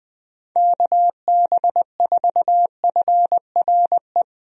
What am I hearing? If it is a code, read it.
Morse code, heard as a keyed tone, about 20 wpm, 700 Hz. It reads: KB4FRE
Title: USA Call Signs - 6 Characters - 20wpm